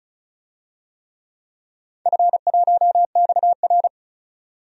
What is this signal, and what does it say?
Morse code, heard as a keyed tone, about 36 wpm, 690 Hz. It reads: F1XR
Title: DXCC-100 Call Signs - 4 Characters - 35wpm